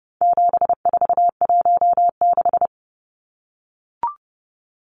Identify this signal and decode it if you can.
Morse code, heard as a keyed tone, about 30 wpm, 700 Hz. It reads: 7416
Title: Numbers - 4-Digit 30wpm